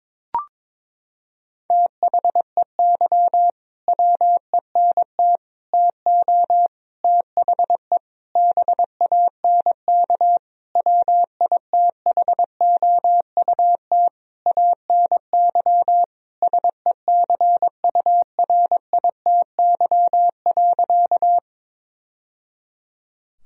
Morse code, heard as a keyed tone, about 21 wpm, 700 Hz. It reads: THEY WENT TO THE BANK WITHOUT ANY SECURITY.
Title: Sentences from Top 1000 Words - Review Part 4 - 22wpm